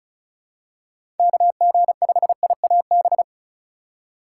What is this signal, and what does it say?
Morse code, heard as a keyed tone, about 34 wpm, 700 Hz. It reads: KG5IAB
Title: USA Call Signs - 6 Characters - 2x Character Spacing - 35wpm